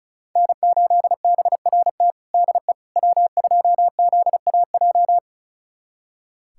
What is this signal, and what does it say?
Morse code, heard as a keyed tone, about 36 wpm, 700 Hz. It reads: N8BRT DE W2ZAJ
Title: USA Callsign DE Callsign - 5 Characters - 35wpm